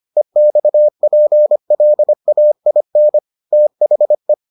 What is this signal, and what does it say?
Morse code, heard as a keyed tone, about 25 wpm, 600 Hz. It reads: EXPLAIN THE